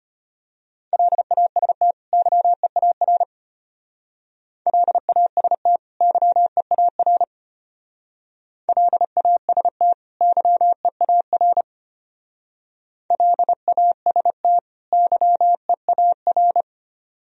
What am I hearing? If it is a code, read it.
Morse code, heard as a keyed tone, about 32 wpm, 700 Hz. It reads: LAST YEAR LAST YEAR LAST YEAR LAST YEAR